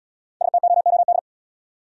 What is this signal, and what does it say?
Morse code, heard as a keyed tone, about 66 wpm, 700 Hz. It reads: STOOD